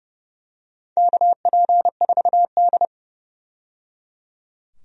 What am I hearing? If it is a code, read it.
Morse code, heard as a keyed tone, about 30 wpm, 700 Hz. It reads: KP4D
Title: DXCC-100 Call Signs - 4 Characters - 30wpm